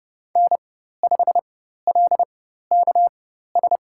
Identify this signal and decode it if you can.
Morse code, heard as a keyed tone, about 29 wpm, spaced as for 14 wpm, 700 Hz. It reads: N5LKS
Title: USA Call Signs - 5 Characters - 4x Character Spacing - 30wpm